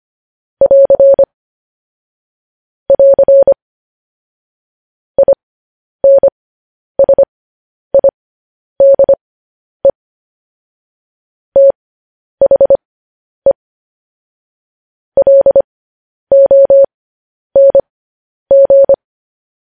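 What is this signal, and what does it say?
Morse code, heard as a keyed tone, about 25 wpm, spaced as for 10 wpm, 550 Hz. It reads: <AR> <AR> INSIDE THE LONG
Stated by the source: Morse code news headlines practice transmission.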